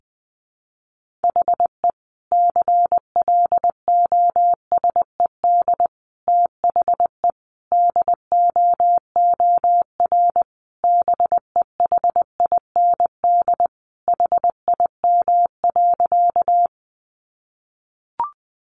Morse code, heard as a keyed tone, about 20 wpm, 700 Hz. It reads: HE CLOSED THE DOOR BEHIND HIM.
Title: Sentences from Top 1400 Words 20wpm